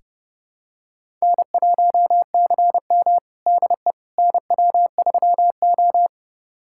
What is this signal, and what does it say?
Morse code, heard as a keyed tone, about 30 wpm, 700 Hz. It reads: N1CM DE NW3O
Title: USA Callsign DE Callsign - 4 Characters - 30wpm